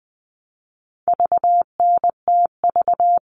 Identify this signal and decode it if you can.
Morse code, heard as a keyed tone, about 20 wpm, 700 Hz. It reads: VNTV
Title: All Cut Numbers - 4-Digit 20wpm